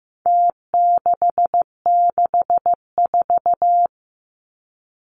Morse code, heard as a keyed tone, about 15 wpm, 700 Hz. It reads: T664
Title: Numbers - Contest Style - 4-Digit 15wpm